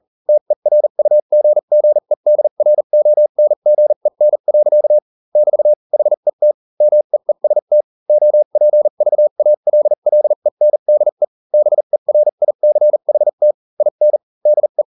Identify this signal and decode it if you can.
Morse code, heard as a keyed tone, about 40 wpm, spaced as for 35 wpm, 600 Hz. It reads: TERUGGEDRONGEN. <BT> HET MEEST OPVALLENDE BERICHT IN DE